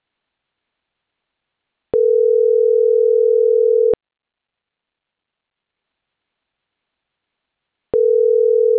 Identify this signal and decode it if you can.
Ringback tone (North America), 440 + 480 Hz.